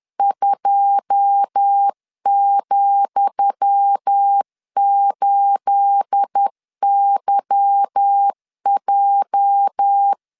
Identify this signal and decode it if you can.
Morse code, heard as a keyed tone, about 10 wpm, 800 Hz. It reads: 2,8YJ
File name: LEZIONI / 10 SIGNS / 10D COMA 10wpm 3013